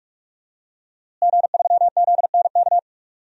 Morse code, heard as a keyed tone, about 42 wpm, 700 Hz. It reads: G3ZNK
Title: DXCC-100 Call Signs - 5 Characters - Repeated 3x - 45wpm